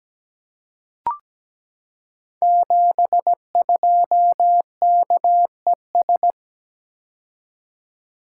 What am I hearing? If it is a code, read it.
Morse code, heard as a keyed tone, about 17 wpm, 700 Hz. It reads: 72KES
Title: Code Groups - 5 Letter:Numbers - 17wpm